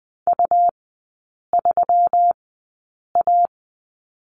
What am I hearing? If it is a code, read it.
Morse code, heard as a keyed tone, about 20 wpm, spaced as for 8 wpm, 700 Hz. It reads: U3A
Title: 121 Lesson 20e - Number 3 - Three Characters 20wpm